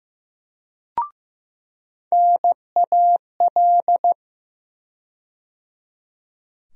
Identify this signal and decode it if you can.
Morse code, heard as a keyed tone, about 15 wpm, 700 Hz. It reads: NAL
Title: USA Call Sign Postfixes - 3 Letters 15wpm